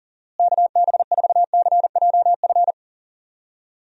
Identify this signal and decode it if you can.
Morse code, heard as a keyed tone, about 38 wpm, 700 Hz. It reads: KB4CJF
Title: USA Call Signs - 6 Characters - Repeated 2x - 40wpm